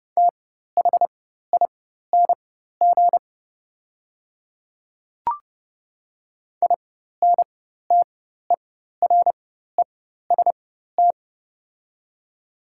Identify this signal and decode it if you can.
Morse code, heard as a keyed tone, about 30 wpm, spaced as for 14 wpm, 700 Hz. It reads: THING INTEREST
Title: 4x Character Spacing - Top 500 Words - 30wpm